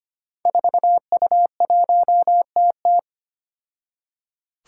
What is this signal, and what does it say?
Morse code, heard as a keyed tone, about 25 wpm, 700 Hz. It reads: 4U1TT
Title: Five Challenging Letter-Number Combinations 25wpm